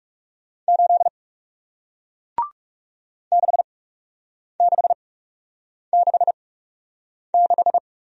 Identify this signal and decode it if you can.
Morse code, heard as a keyed tone, about 40 wpm, spaced as for 8 wpm, 700 Hz. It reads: 8 6666